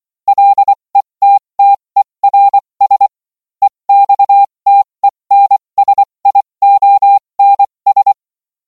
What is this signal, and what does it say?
Morse code, heard as a keyed tone, about 24 wpm, spaced as for 20 wpm, 800 Hz. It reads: LETTERS EXTENSIONS